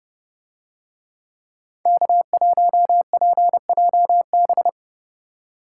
Morse code, heard as a keyed tone, about 30 wpm, 700 Hz. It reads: K1PJB